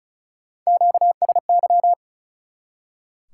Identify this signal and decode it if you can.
Morse code, heard as a keyed tone, about 36 wpm, 700 Hz. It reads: QSY
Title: Top 180 QSO Elements 35wpm